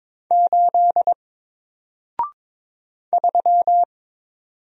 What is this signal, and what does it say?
Morse code, heard as a keyed tone, about 22 wpm, 700 Hz. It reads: 8 3
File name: Single-Letter-Number-22wpm